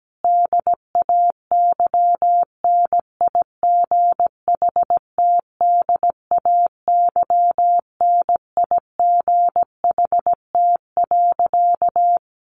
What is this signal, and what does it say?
Morse code, heard as a keyed tone, about 17 wpm, 700 Hz. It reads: DAY NIGHT DAY NIGHT.
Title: A Sound Of Thunder 17wpm